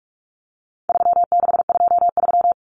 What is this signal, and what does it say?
Morse code, heard as a keyed tone, about 44 wpm, 700 Hz. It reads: 3623
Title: Numbers - 4-Digit 45wpm